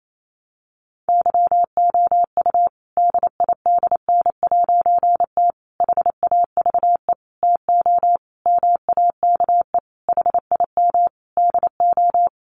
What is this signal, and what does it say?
Morse code, heard as a keyed tone, about 28 wpm, 700 Hz. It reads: YOU DIDN'T HAVE TO MAKE HIM DO
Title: A Sound Of Thunder 28wpm